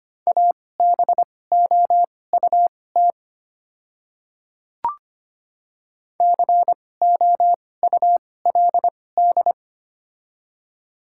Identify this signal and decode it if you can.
Morse code, heard as a keyed tone, about 25 wpm, spaced as for 18 wpm, 700 Hz. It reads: ABOUT COULD
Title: 2x Character Spacing - Top 100 Words - 25wpm